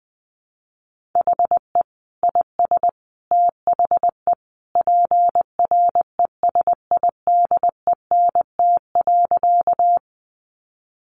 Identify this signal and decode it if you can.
Morse code, heard as a keyed tone, about 20 wpm, 700 Hz. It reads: HE IS THE PRESIDENT.
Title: Sentences from Top 1000 Words 20wpm